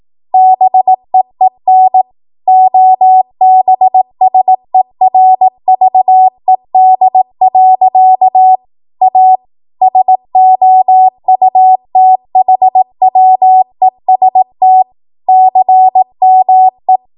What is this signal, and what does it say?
Morse code, heard as a keyed tone, about 18 wpm, 750 Hz. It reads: BEEN OBSERVED. A SOUTHWEST CME
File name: Morse / 18WPM